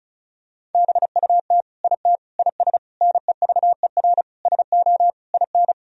Morse code, heard as a keyed tone, about 36 wpm, 700 Hz. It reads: BUT IT IS NEVER SO IN
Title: Sentences from Top 1000 Words - Review Part 2 - 35wpm